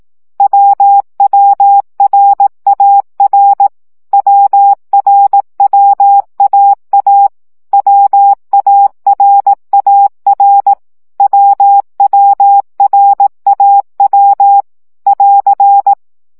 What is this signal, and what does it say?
Morse code, heard as a keyed tone, about 18 wpm, 800 Hz. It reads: WWRAR WRWAA WARAR WWRAW <AR>
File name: CW gyakorlas a r w 15 baud